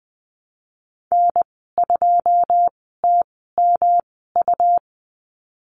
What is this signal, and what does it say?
Morse code, heard as a keyed tone, about 19 wpm, spaced as for 14 wpm, 700 Hz. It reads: N2TMU
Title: 147 Lesson 24g - Letter y - Call Signs 20wpm